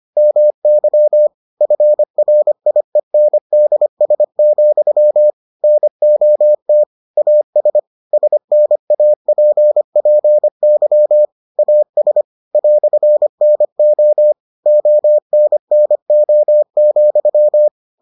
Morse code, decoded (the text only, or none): MY FRIENDS, NOT AS SNAPPY AS "NO ONNO,